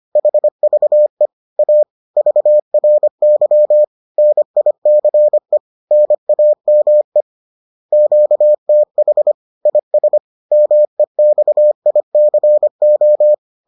Morse code, heard as a keyed tone, about 25 wpm, 600 Hz. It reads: HVE A VRY NICE NAME QTH IS MEXICO